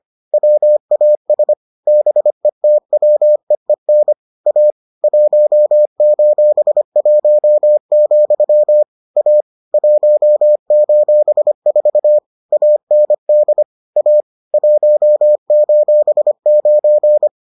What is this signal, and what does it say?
Morse code, heard as a keyed tone, about 25 wpm, 600 Hz. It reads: WAS BETWEEN A 1:1, A 1:4 AND A 1:9